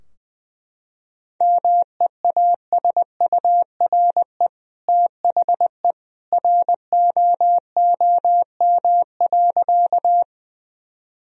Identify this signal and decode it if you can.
Morse code, heard as a keyed tone, about 21 wpm, 700 Hz. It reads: MEASURE THE ROOM.
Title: Sentences from Top 1000 Words - Review Part 2 - 20wpm